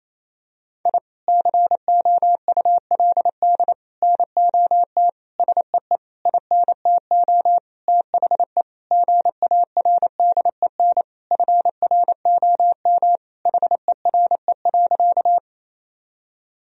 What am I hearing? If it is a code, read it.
Morse code, heard as a keyed tone, about 28 wpm, 700 Hz. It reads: I COULD NOT SEE INTO THE GARDEN FROM HERE.
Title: Sentences from Top 1000 Words - Review Part 4 - 28wpm